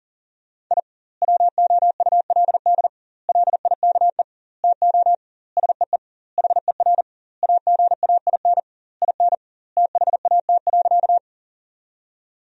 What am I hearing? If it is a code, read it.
Morse code, heard as a keyed tone, about 40 wpm, 700 Hz. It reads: I WOULD LIKE TO SEE HER AGAIN IN THAT.
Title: 2x Word-Spacing - Sentences from Top 200 Words - 40wpm